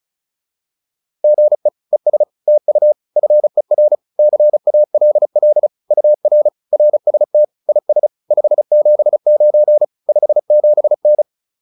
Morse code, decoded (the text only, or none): GE ES TU FER CALL UR RST IS 579 57N